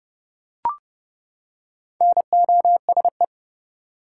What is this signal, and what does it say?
Morse code, heard as a keyed tone, about 30 wpm, 700 Hz. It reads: NOSE
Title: Four Letter Words - 30wpm